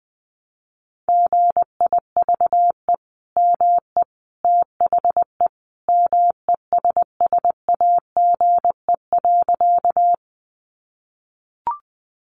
Morse code, heard as a keyed tone, about 20 wpm, 700 Hz. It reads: GIVE ME THE MESSAGE.
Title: Sentences from Top 1100 Words 20wpm